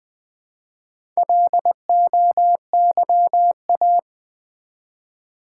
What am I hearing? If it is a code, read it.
Morse code, heard as a keyed tone, about 20 wpm, 700 Hz. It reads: LOYA
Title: Build A Word - 1 Letter At A Time - 3 to 5-Letter Words - 20wpm